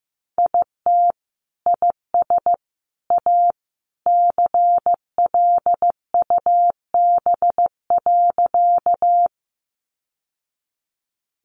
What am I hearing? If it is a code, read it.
Morse code, heard as a keyed tone, about 15 wpm, 700 Hz. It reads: IT IS A CLUB.
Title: Sentences from Top 1200 Words 15wpm